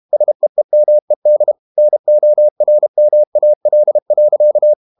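Morse code, heard as a keyed tone, about 32 wpm, 600 Hz. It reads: SEEMED NORMAL.
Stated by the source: Morse code practice at 32 wpm from CW med Gnister, reading Ken Follett's novel Never in English.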